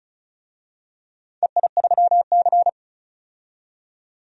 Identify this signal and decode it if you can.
Morse code, heard as a keyed tone, about 36 wpm, 700 Hz. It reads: EI3C
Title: DXCC-100 Call Signs - 4 Characters - 35wpm